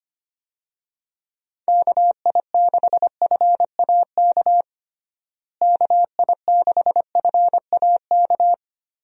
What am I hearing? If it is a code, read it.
Morse code, heard as a keyed tone, about 25 wpm, 700 Hz. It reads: KI6FAK KI6FAK
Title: USA Call Signs - 6 Characters - Repeated 3x - 25wpm